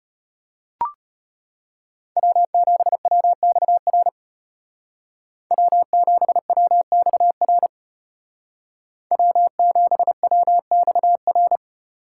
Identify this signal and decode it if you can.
Morse code, heard as a keyed tone, about 31 wpm, 700 Hz. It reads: W7WXR W7WXR W7WXR